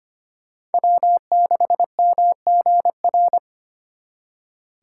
Morse code, heard as a keyed tone, about 33 wpm, 700 Hz. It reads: W6MGR